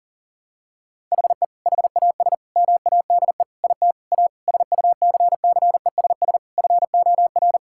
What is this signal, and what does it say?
Morse code, heard as a keyed tone, about 40 wpm, 700 Hz. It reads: HE HAS MADE IT A SUCCESS FOR
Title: Sentences from Top 1000 Words - Review Part 6 - 40wpm